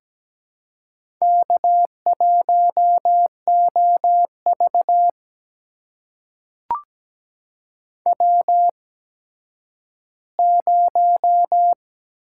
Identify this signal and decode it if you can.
Morse code, heard as a keyed tone, about 17 wpm, spaced as for 10 wpm, 700 Hz. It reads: K1OV W0
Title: USA Call Signs - 4 Characters - 8x Character Spacing - 17wpm